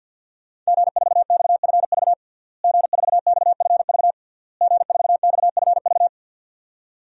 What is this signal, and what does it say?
Morse code, heard as a keyed tone, about 47 wpm, 700 Hz. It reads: G4XFV G4XFV G4XFV
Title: DXCC-100 Call Signs - 5 Characters - Repeated 3x - 50wpm